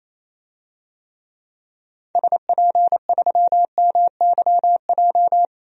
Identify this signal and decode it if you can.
Morse code, heard as a keyed tone, about 28 wpm, 700 Hz. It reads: SP3MYJ
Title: DXCC-100 Call Signs - 6 Characters - Repeated 3x - 28wpm